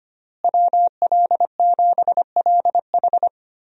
Morse code, heard as a keyed tone, about 25 wpm, 700 Hz. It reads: WL7LH
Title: USA Call Signs - 5 Characters - Repeated 2x - 25wpm